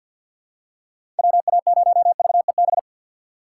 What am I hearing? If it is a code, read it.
Morse code, heard as a keyed tone, about 47 wpm, 700 Hz. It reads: WA0VEB
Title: USA Call Signs - 6 Characters - 6x Character Spacing - 50wpm